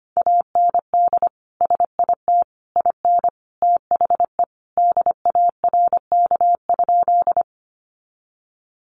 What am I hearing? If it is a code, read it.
Morse code, heard as a keyed tone, about 25 wpm, 700 Hz. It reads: AND SIT IN THE DARK?
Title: Sentences from Top 500 Words 25wpm